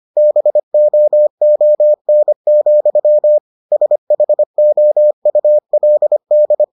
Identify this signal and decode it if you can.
Morse code, heard as a keyed tone, about 25 wpm, 600 Hz. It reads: BOON, SHOULD